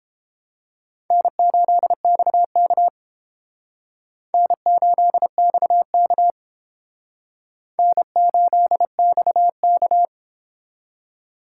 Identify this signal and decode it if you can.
Morse code, heard as a keyed tone, about 32 wpm, 700 Hz. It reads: N8XK N8XK N8XK